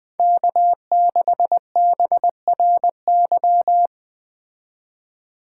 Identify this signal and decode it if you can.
Morse code, heard as a keyed tone, about 20 wpm, 700 Hz. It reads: K6BRY